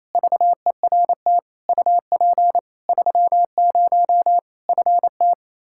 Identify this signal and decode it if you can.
Morse code, heard as a keyed tone, about 28 wpm, 700 Hz. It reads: VERT UP 30 FT
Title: Practice QSO - Exchange 2 - Info - 28wpm